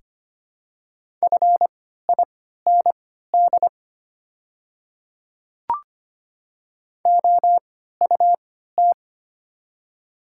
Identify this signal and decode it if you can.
Morse code, heard as a keyed tone, about 25 wpm, spaced as for 14 wpm, 690 Hz. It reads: FIND OUT
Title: 3x Character Spacing - Top 100 Words - 25wpm